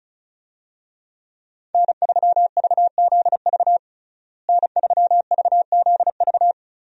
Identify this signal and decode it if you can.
Morse code, heard as a keyed tone, about 36 wpm, 700 Hz. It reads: N3VZV N3VZV
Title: USA Call Signs - 5 Characters - Repeated 2x - 35wpm